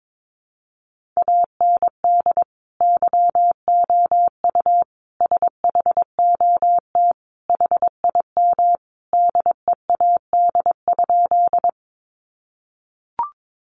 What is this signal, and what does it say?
Morse code, heard as a keyed tone, about 22 wpm, 700 Hz. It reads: AND YOU SHOT HIM DEAD?
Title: Sentences from Top 1500 Words 22wpm